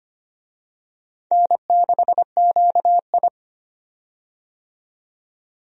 Morse code, heard as a keyed tone, about 25 wpm, 700 Hz. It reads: N6QI
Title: USA Call Signs - 4 Characters - 25wpm